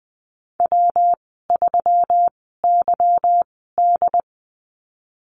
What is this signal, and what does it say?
Morse code, heard as a keyed tone, about 20 wpm, spaced as for 15 wpm, 700 Hz. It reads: W3YD
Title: 167 Lesson 27g - Number 7 - Call Signs 20wpm